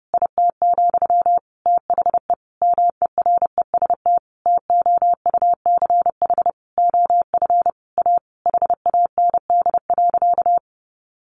Morse code, heard as a keyed tone, about 30 wpm, 700 Hz. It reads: IT, THE MEREST TOUCH OF A HAND.
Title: A Sound Of Thunder 30wpm